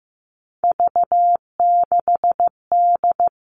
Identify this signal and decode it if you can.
Morse code, heard as a keyed tone, about 15 wpm, 700 Hz. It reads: V6D
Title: All Cut Numbers - 3-Digit 15wpm